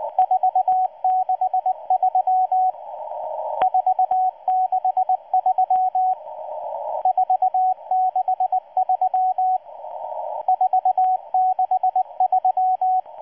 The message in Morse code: 463 463 463 463